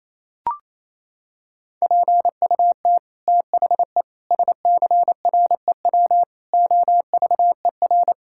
Morse code, heard as a keyed tone, about 28 wpm, 700 Hz. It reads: PUT THE SCREW OVER
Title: Sentences from Top 1600 Words 28wpm